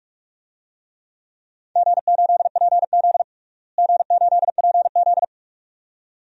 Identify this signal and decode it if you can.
Morse code, heard as a keyed tone, about 46 wpm, 700 Hz. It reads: G8PZ G8PZ
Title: DXCC-100 Call Signs - 4 Characters - Repeated 2x - 45wpm